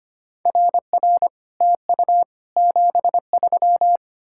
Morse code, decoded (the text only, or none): RR TU 73